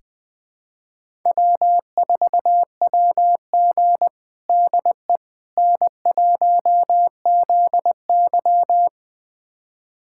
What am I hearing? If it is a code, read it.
Morse code, heard as a keyed tone, about 20 wpm, 700 Hz. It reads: W4WG DE N1ZY
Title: USA Callsign DE Callsign - 4 Characters - 20wpm